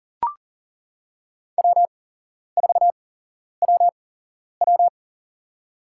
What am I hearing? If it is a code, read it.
Morse code, heard as a keyed tone, about 40 wpm, spaced as for 11 wpm, 700 Hz. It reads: W4WW
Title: USA Call Signs - 4 Characters - 8x Character Spacing - 40wpm